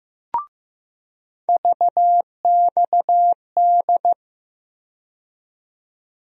Morse code, heard as a keyed tone, about 15 wpm, 700 Hz. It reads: VXD
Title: USA Call Sign Postfixes - 3 Letters 15wpm